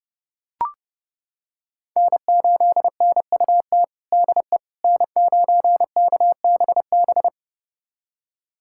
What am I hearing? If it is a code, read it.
Morse code, heard as a keyed tone, about 30 wpm, 700 Hz. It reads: N8NUT DE N9KBB
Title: USA Callsign DE Callsign - 5 Characters - 30wpm